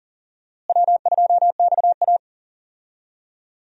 Morse code, heard as a keyed tone, about 37 wpm, 700 Hz. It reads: W2XA
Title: USA Call Signs - 4 Characters - Repeated 2x - 40wpm